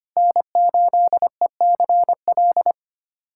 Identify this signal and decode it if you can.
Morse code, heard as a keyed tone, about 25 wpm, 700 Hz. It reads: N8ECL
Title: USA Callsign DE Callsign - 5 Characters - 25wpm